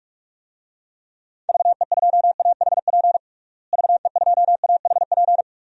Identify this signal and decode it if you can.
Morse code, heard as a keyed tone, about 41 wpm, 700 Hz. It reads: VE2AHP VE2AHP
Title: DXCC-100 Call Signs - 6 Characters - Repeated 2x - 45wpm